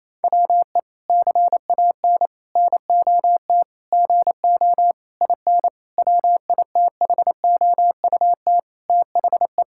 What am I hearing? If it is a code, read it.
Morse code, heard as a keyed tone, about 27 wpm, 700 Hz. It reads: WE CAN NOT GO IN WITHOUT THE
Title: Sentences from Top 1000 Words - Review Part 6 - 28wpm